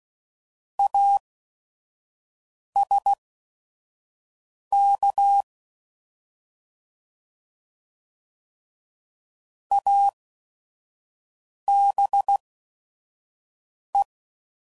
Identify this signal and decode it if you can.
Morse code, heard as a keyed tone, about 16 wpm, spaced as for 5 wpm, 800 Hz. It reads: ASK ABE
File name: Prefixes / By First Letter / As